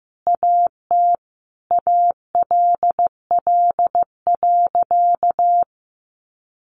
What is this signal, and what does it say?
Morse code, heard as a keyed tone, about 15 wpm, 700 Hz. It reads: AT ALL.
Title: Sentences from Top 500 Words 15wpm